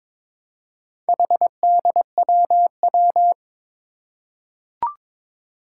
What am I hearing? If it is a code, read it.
Morse code, heard as a keyed tone, about 22 wpm, 700 Hz. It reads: HDWW
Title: Four Challenging Letter-Number Combinations 22wpm